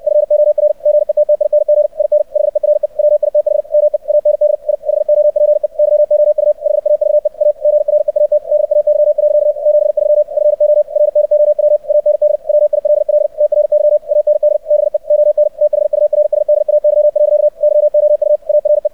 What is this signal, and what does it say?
Morse code, heard as a keyed tone, about 64 wpm, 600 Hz. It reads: FOR GETTING IN HERE WITH ME AND I HOPE YOU HAVE A GREAT DAY. 73 FOR NOW AND WILL TRY AND BE ON THURSDAY. YOU ARE
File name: archived / CRSnet